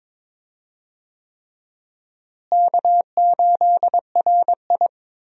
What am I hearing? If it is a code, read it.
Morse code, heard as a keyed tone, about 22 wpm, 700 Hz. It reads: K8RI
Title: USA Call Signs - 4 Characters - Repeated 3x - 22wpm